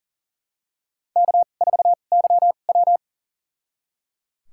Morse code, heard as a keyed tone, about 37 wpm, spaced as for 28 wpm, 700 Hz. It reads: K4YW
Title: USA Call Signs - 4 Characters - 2x Character Spacing - 40wpm